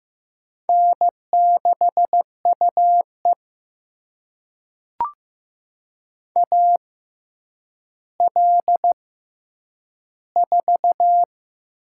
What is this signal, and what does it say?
Morse code, heard as a keyed tone, about 15 wpm, spaced as for 9 wpm, 700 Hz. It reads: N6UE AL4
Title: USA Call Signs - 4 Characters - 6x Character Spacing - 15wpm